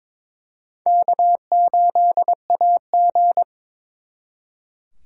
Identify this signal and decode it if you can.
Morse code, heard as a keyed tone, about 26 wpm, 700 Hz. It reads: K8AG
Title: Speed Racing - USA 1x2 Call Signs 22wpm